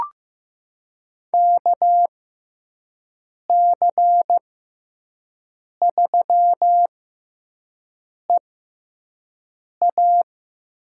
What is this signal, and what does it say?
Morse code, heard as a keyed tone, about 15 wpm, spaced as for 5 wpm, 700 Hz. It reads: KC3EA
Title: USA Call Signs - 6 Characters - 6x Character Spacing - 15wpm